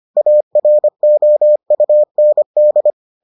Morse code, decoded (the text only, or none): AROUND